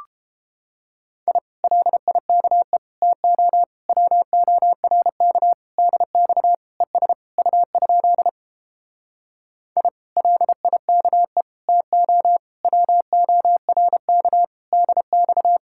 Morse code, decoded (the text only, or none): I LIKE TO WORK DX ES U? I LIKE TO WORK DX